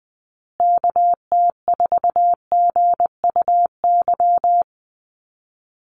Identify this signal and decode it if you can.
Morse code, heard as a keyed tone, about 20 wpm, 700 Hz. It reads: KT4GUY
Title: USA Call Signs Repeated 2x 20wpm